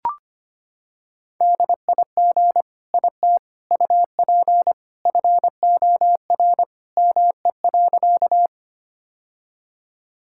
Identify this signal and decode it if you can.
Morse code, heard as a keyed tone, about 25 wpm, 700 Hz. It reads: DIG IT UP FOR ME.
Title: Sentences from Top 1600 Words 25wpm